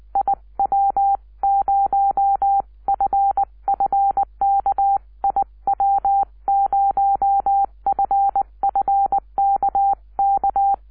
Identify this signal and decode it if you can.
Morse code, heard as a keyed tone, about 20 wpm, 780 Hz. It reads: IW0FFK IW0FFK K